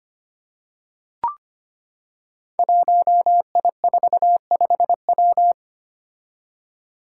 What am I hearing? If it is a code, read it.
Morse code, heard as a keyed tone, about 25 wpm, 700 Hz. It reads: 1I45W
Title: Five Challenging Letter-Number Combinations 25wpm